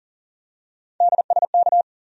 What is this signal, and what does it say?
Morse code, heard as a keyed tone, about 40 wpm, 700 Hz. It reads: DSK
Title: USA Call Sign Postfixes - 3 Letters 40wpm